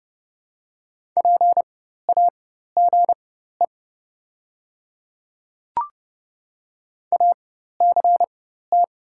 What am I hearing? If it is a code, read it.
Morse code, heard as a keyed tone, about 30 wpm, spaced as for 14 wpm, 700 Hz. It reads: PAGE ACT
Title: 4x Character Spacing - Top 500 Words - 30wpm